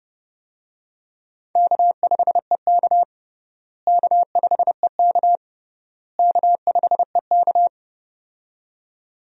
Morse code, heard as a keyed tone, about 32 wpm, 700 Hz. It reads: K5EK K5EK K5EK
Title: USA Call Signs - 4 Characters - Repeated 3x - 30wpm